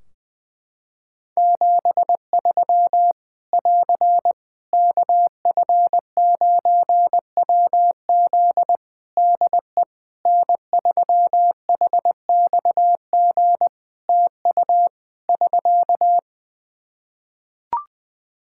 Morse code, heard as a keyed tone, about 20 wpm, 700 Hz. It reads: 73 <AR> KF9WZ DE N3HXG TU <SK>
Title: Practice QSO - Exchange 4 - Ending - 20wpm